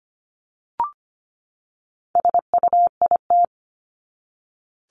Morse code, heard as a keyed tone, about 25 wpm, 700 Hz. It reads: SUIT
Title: Top 1501-1750 Words 25wpm